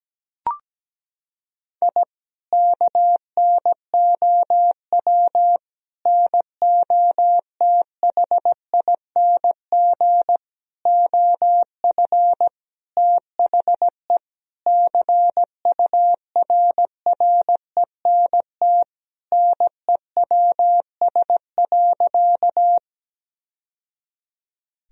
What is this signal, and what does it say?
Morse code, heard as a keyed tone, about 17 wpm, 700 Hz. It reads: I KNOW NOTHING OF THE CURRENT NEWS.
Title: Sentences from Top 1000 Words - Review Part 4 - 17wpm